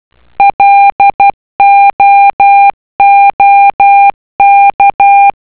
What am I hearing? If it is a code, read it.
Morse code, heard as a keyed tone, about 12 wpm, 800 Hz. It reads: LOOK